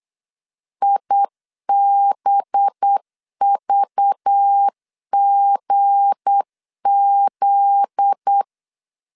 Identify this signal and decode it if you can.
Morse code, heard as a keyed tone, about 8 wpm, 800 Hz. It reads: IBVGZ
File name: LEZIONI / 7 GF9Z / 7 GF9Z 8wpm 3213